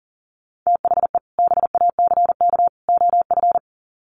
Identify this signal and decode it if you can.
Morse code, heard as a keyed tone, about 40 wpm, 700 Hz. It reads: THE BACK OF
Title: Sets of 3 Words 40wpm